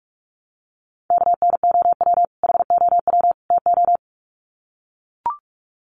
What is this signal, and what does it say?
Morse code, heard as a keyed tone, about 44 wpm, 700 Hz. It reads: KNOW HOW TO
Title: Sets of 3 Words 45wpm